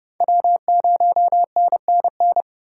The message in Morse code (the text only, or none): W0NNN